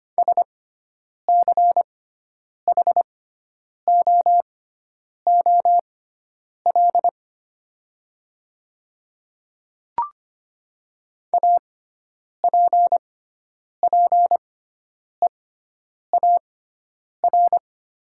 Morse code, heard as a keyed tone, about 25 wpm, spaced as for 9 wpm, 700 Hz. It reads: SCHOOL APPEAR
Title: 6x Character Spacing - Top 500 Words - 25wpm